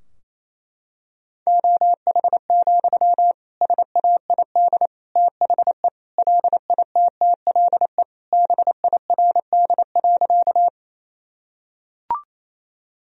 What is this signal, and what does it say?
Morse code, heard as a keyed tone, about 28 wpm, 700 Hz. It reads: OH, SAID THE LITTLE BIRD.
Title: Sentences from Top 1000 Words - Review Part 2 - 28wpm